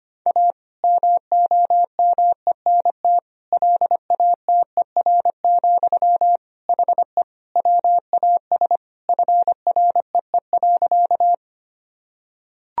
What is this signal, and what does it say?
Morse code, heard as a keyed tone, about 25 wpm, 700 Hz. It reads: A MOMENT LATER, HE WAS FREE.
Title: Sentences from Top 1000 Words - Review Part 4 - 25wpm